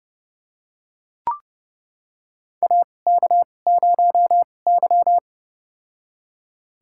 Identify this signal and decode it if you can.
Morse code, heard as a keyed tone, about 31 wpm, spaced as for 22 wpm, 700 Hz. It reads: AK0Y
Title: USA Call Signs - 2x Character Spacing - 30wpm